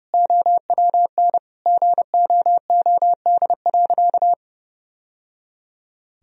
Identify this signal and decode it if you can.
Morse code, heard as a keyed tone, about 32 wpm, 700 Hz. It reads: OWN GOOD.
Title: Sentences from Top 1000 Words - Review Part 4 - 30wpm